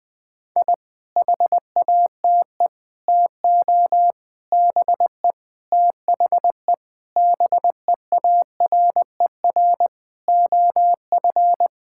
Morse code, heard as a keyed tone, about 20 wpm, 700 Hz. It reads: I HATE TO BE THE BEARER OF
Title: Sets of 9 Words - 20wpm